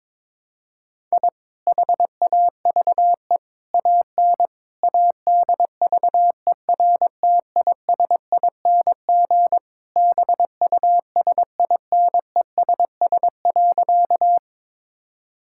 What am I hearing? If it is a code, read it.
Morse code, heard as a keyed tone, about 22 wpm, 700 Hz. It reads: I HAVE AN ADVERTISING BUSINESS.
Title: Sentences from Top 1000 Words - Review Part 6 - 22wpm